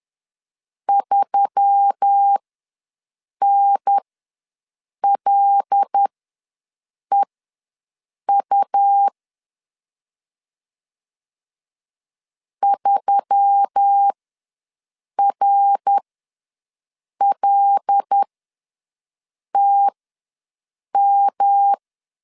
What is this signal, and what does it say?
Morse code, heard as a keyed tone, about 10 wpm, spaced as for 6 wpm, 800 Hz. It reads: 3NLEU 3RLTM
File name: LEZIONI / 5 LBVC / 5 L ws 10wpm 3636